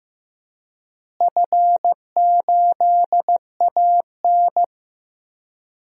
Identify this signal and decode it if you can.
Morse code, heard as a keyed tone, about 15 wpm, 700 Hz. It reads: F8AN
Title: DXCC-100 Call Signs - 4 Characters - Repeated 2x - 15wpm